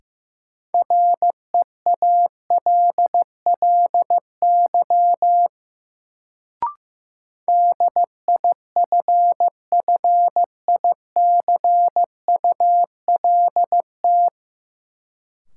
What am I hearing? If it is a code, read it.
Morse code, heard as a keyed tone, about 16 wpm, 700 Hz. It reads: REALLY DIFFICULT
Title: Sentences from Top 600 Words - New Word Drill - 15wpm